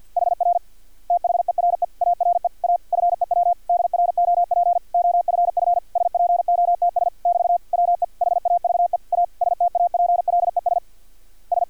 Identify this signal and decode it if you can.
Morse code, heard as a keyed tone, about 50 wpm, 700 Hz. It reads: HR THERE ARE A FEW DROP OFF SPOTS <BT> WE HAVE A STAPLES S
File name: archived / CRSnet